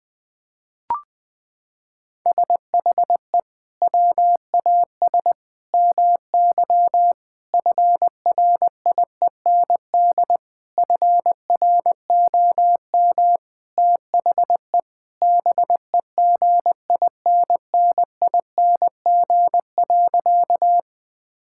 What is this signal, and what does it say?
Morse code, heard as a keyed tone, about 19 wpm, 700 Hz. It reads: SHE WAS MY FRIEND FROM THE BEGINNING.
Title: Sentences from Top 1000 Words - Review Part 5 - 20wpm